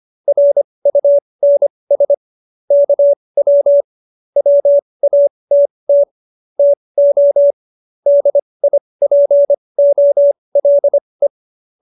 Morse code, heard as a keyed tone, about 25 wpm, spaced as for 20 wpm, 570 Hz. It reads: RUNS KW WATT TO DIPOLE